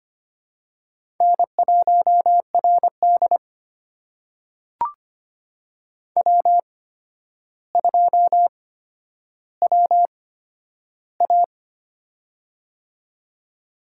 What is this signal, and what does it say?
Morse code, heard as a keyed tone, about 25 wpm, spaced as for 11 wpm, 700 Hz. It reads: N1RD W2WA
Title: USA Call Signs - 4 Characters - 8x Character Spacing - 25wpm